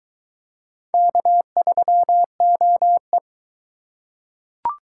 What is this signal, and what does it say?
Morse code, heard as a keyed tone, about 22 wpm, 700 Hz. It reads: K3OE